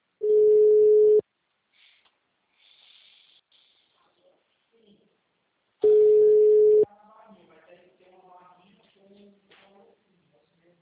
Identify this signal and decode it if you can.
Ringback tone (Europe, 425 Hz), with speech.
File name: out- -2013- - - .66564